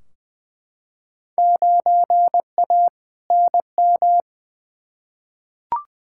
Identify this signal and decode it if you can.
Morse code, heard as a keyed tone, about 20 wpm, 700 Hz. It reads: 9A NM
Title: Field Day Exchanges Encore - 20wpm